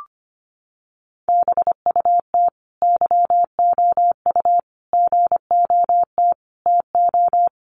Morse code, heard as a keyed tone, about 25 wpm, 700 Hz. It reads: BUT YOU GOT TO
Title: Sentences from Top 300 Words 25wpm